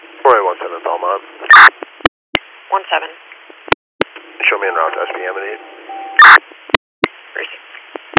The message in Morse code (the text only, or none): none